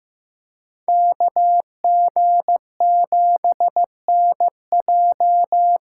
Morse code, heard as a keyed tone, about 15 wpm, 700 Hz. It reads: KG7NJ
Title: USA Call Signs - 6x Character Spacing - 15wpm